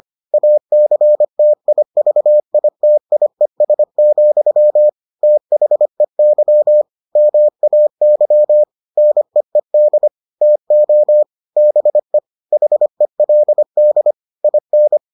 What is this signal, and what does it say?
Morse code, heard as a keyed tone, about 25 wpm, 600 Hz. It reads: ACTIVITIES, THEY MAY NEED TO BE HELD IN